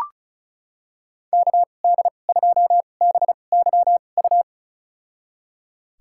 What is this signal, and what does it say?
Morse code, heard as a keyed tone, about 36 wpm, spaced as for 25 wpm, 700 Hz. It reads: KD2BYU
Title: USA Call Signs - 6 Characters - 2x Character Spacing - 35wpm